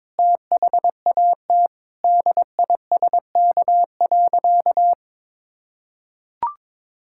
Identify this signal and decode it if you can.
Morse code, heard as a keyed tone, about 22 wpm, 700 Hz. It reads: THAT DISK.
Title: Sentences from Top 1000 Words - Review Part 6 - 22wpm